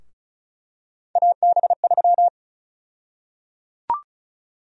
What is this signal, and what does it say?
Morse code, heard as a keyed tone, about 36 wpm, 700 Hz. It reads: AB3
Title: USA Call Sign Prefixes - 35wpm